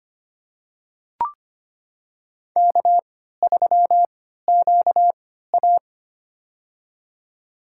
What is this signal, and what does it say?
Morse code, heard as a keyed tone, about 25 wpm, spaced as for 14 wpm, 700 Hz. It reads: K3QA
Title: USA Call Signs - 3x Character Spacing - 25wpm